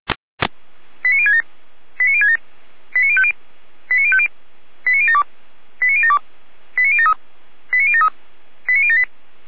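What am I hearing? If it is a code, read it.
Five-tone selective call (ZVEI), heard as quick runs of five tones, tones of about 70 ms. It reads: 89067 89067 89050 89050 89072 89072 89073 89073 89078
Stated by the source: ZVEI five-tone call received on an SDR.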